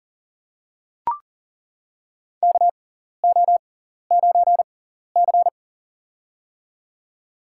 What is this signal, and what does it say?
Morse code, heard as a keyed tone, about 38 wpm, spaced as for 14 wpm, 700 Hz. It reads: KO9C
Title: USA Call Signs - 4 Characters - 6x Character Spacing - 40wpm